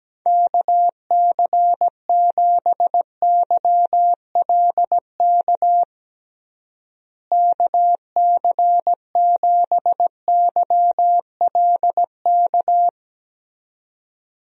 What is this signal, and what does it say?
Morse code, heard as a keyed tone, about 17 wpm, 700 Hz. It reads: KC7YLK KC7YLK
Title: USA Call Signs - 6 Characters - Repeated 2x - 17wpm